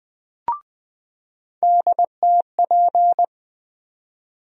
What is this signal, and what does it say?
Morse code, heard as a keyed tone, about 20 wpm, 700 Hz. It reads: DTP
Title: USA Call Sign Postfixes - 3 Letters 20wpm